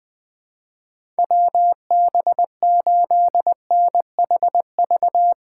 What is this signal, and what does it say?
Morse code, heard as a keyed tone, about 20 wpm, 700 Hz. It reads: WB8NHV
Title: USA Call Signs - 6 Characters - 8x Character Spacing - 20wpm